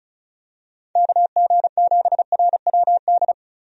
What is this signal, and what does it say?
Morse code, heard as a keyed tone, about 36 wpm, 700 Hz. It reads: KG7RWD
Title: USA Call Signs - 6 Characters - 35wpm